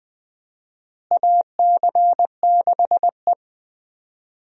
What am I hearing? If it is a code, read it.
Morse code, heard as a keyed tone, about 20 wpm, 700 Hz. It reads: AC6E
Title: USA Call Signs - 4 Characters - Repeated 2x - 20wpm